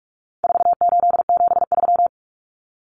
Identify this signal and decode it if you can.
Morse code, heard as a keyed tone, about 44 wpm, 700 Hz. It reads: 4873
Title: Common Cut Numbers - 4-Digit 45wpm